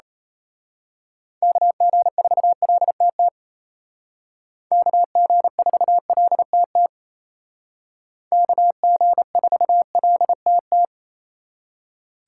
Morse code, heard as a keyed tone, about 33 wpm, 700 Hz. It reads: KG4LTT KG4LTT KG4LTT
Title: Speed Racing - USA 2x3 Call Signs 28wpm